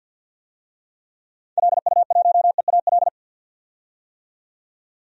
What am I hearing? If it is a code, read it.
Morse code, heard as a keyed tone, about 47 wpm, 700 Hz. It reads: LU1EAL
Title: DXCC-100 Call Signs - 6 Characters - 50wpm